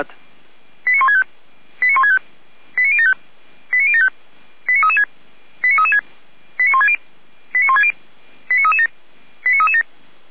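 Five-tone selective call (ZVEI), heard as quick runs of five tones, tones of about 70 ms. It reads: 89165 89165 89075 89075 89336 89336 89160 89160 89337 89337
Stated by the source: ZVEI five-tone call received by SDR.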